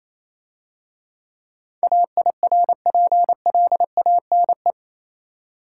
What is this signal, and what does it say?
Morse code, heard as a keyed tone, about 27 wpm, 700 Hz. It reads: AIRPLANE
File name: Easy Words - Speed-Racing with Hints - 22wpm